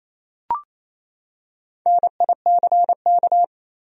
Morse code, heard as a keyed tone, about 28 wpm, 700 Hz. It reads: NICK
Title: USA Common Names - Max 4 Letters - 28wpm